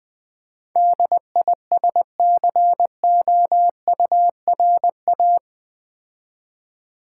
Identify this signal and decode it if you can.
Morse code, heard as a keyed tone, about 20 wpm, 700 Hz. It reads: DISCOURA
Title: Build A Word - 4 Letters At A Time - 10-Letter Words - 20wpm